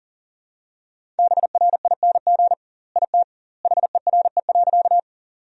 Morse code, heard as a keyed tone, about 40 wpm, 700 Hz. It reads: BRING IT HERE.
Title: 2x Word-Spacing - Sentences from Top 500 Words - 40wpm